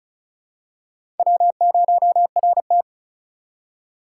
Morse code, heard as a keyed tone, about 36 wpm, 700 Hz. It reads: W0RT
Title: USA Call Signs - 4 Characters - 3x Character Spacing - 35wpm